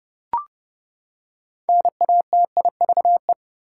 Morse code, heard as a keyed tone, about 32 wpm, 700 Hz. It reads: NATIVE
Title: Six Letter Words - 30wpm